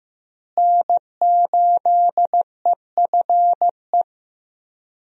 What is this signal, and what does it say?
Morse code, heard as a keyed tone, about 15 wpm, 700 Hz. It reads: N8EFE
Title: USA Call Signs - 5 Characters - Repeated 3x - 15wpm